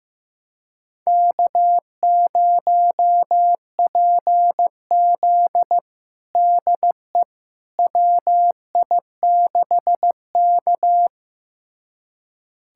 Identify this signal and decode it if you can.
Morse code, heard as a keyed tone, about 15 wpm, 700 Hz. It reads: K0PZ DE WI6K
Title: USA Callsign DE Callsign - 4 Characters - 15wpm